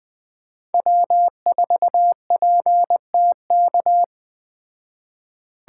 Morse code, heard as a keyed tone, about 21 wpm, 700 Hz. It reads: W4PTK
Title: Speed Racing - USA 1x3 Call Signs 17wpm